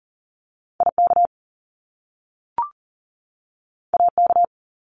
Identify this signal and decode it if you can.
Morse code, heard as a keyed tone, about 40 wpm, 700 Hz. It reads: IK AK
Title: Top Two Letter Combinations in Words 40wpm